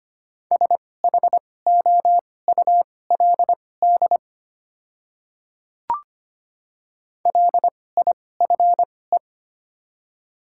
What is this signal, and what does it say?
Morse code, heard as a keyed tone, about 25 wpm, spaced as for 18 wpm, 700 Hz. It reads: SHOULD LIFE
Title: 2x Character Spacing - Top 300 Words - 25wpm